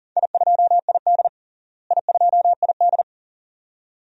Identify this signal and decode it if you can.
Morse code, heard as a keyed tone, about 37 wpm, 700 Hz. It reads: I2ID I2ID
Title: DXCC-100 Call Signs - 4 Characters - Repeated 2x - 40wpm